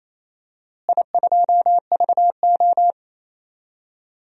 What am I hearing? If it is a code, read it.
Morse code, heard as a keyed tone, about 28 wpm, 700 Hz. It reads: I2VO
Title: DXCC-100 Call Signs - 4 Characters - Repeated 2x - 28wpm